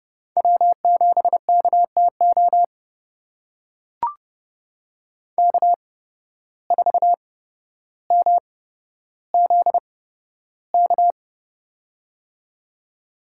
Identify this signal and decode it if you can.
Morse code, heard as a keyed tone, about 29 wpm, spaced as for 13 wpm, 700 Hz. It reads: W7KTO K4MZK
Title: USA Call Signs - 5 Characters - 8x Character Spacing - 30wpm